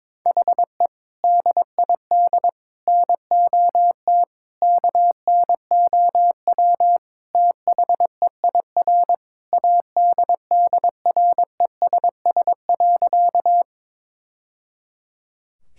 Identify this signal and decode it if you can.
Morse code, heard as a keyed tone, about 22 wpm, 700 Hz. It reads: HE DID NOT KNOW THEIR ADDRESS.
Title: Sentences from Top 1000 Words - Review Part 6 - 22wpm